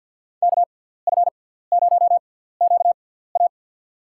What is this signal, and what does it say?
Morse code, heard as a keyed tone, about 48 wpm, spaced as for 17 wpm, 700 Hz. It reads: KF0QA
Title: USA Call Signs - 5 Characters - 6x Character Spacing - 50wpm